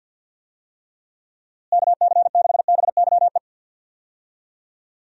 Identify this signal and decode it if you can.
Morse code, heard as a keyed tone, about 46 wpm, 700 Hz. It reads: KC6BYE
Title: USA Call Signs - 6 Characters - Repeated 3x - 50wpm